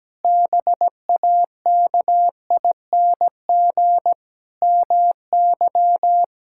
Morse code, heard as a keyed tone, about 17 wpm, 700 Hz. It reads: BAKING MY
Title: Pangram Challenge v2 - 17wpm